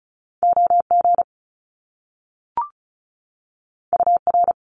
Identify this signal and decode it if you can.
Morse code, heard as a keyed tone, about 36 wpm, 700 Hz. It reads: OG UR
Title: Top Two Letter Combinations in Words 35wpm